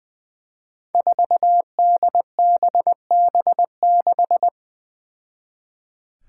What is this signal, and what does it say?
Morse code, heard as a keyed tone, about 20 wpm, 700 Hz. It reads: 4DBB6
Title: Five Challenging Letter-Number Combinations 20wpm